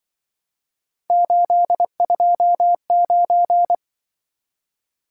Morse code, heard as a keyed tone, about 25 wpm, 700 Hz. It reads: 829